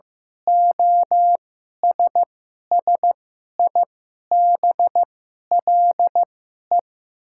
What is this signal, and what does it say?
Morse code, heard as a keyed tone, about 15 wpm, spaced as for 11 wpm, 690 Hz. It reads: OSSIBLE
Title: 2x Character Spacing - Top 500 Words - 15wpm